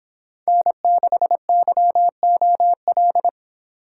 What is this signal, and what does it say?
Morse code, heard as a keyed tone, about 26 wpm, 700 Hz. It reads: N6YOL